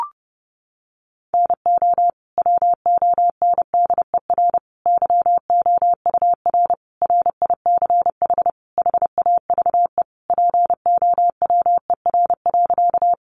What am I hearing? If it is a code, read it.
Morse code, heard as a keyed tone, about 28 wpm, 700 Hz. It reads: NO WONDER YOUR RICH HAVE POWER.
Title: Sentences from Top 1100 Words 30wpm